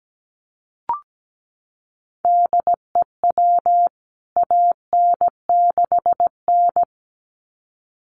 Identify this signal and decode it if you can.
Morse code, heard as a keyed tone, about 17 wpm, 700 Hz. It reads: DEW AN6N
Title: CWT Exchange 17wpm